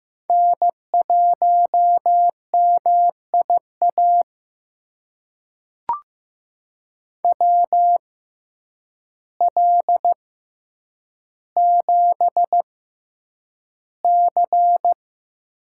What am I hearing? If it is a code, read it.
Morse code, heard as a keyed tone, about 15 wpm, spaced as for 8 wpm, 700 Hz. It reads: N1MIA WL7C
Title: USA Call Signs - 5 Characters - 6x Character Spacing - 15wpm